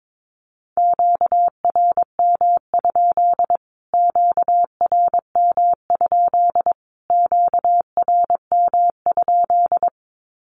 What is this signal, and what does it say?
Morse code, heard as a keyed tone, about 22 wpm, 700 Hz. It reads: QRM? QRM? QRM?
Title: Q-Codes Repeated 3x 22wpm